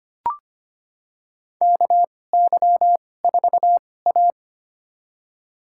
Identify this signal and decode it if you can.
Morse code, heard as a keyed tone, about 25 wpm, spaced as for 18 wpm, 700 Hz. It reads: KY4A
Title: USA Call Signs - 4 Characters - 2x Character Spacing - 25wpm